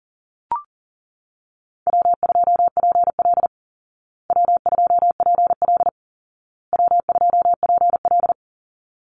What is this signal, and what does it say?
Morse code, heard as a keyed tone, about 40 wpm, 700 Hz. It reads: W2PL W2PL W2PL
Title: Easy Call Signs Repeated 3x - 40wpm